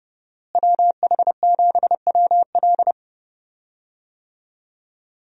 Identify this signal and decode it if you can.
Morse code, heard as a keyed tone, about 30 wpm, 700 Hz. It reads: WH7WL
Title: USA Call Signs - 5 Characters - Repeated 3x - 30wpm